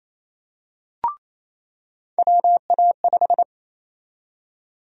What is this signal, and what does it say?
Morse code, heard as a keyed tone, about 28 wpm, 700 Hz. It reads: WA5
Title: USA Call Sign Prefixes - 28wpm